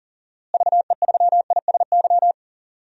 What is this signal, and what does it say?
Morse code, heard as a keyed tone, about 40 wpm, 700 Hz. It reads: VE3ISY
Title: DXCC-100 Call Signs - 6 Characters - Repeated 2x - 40wpm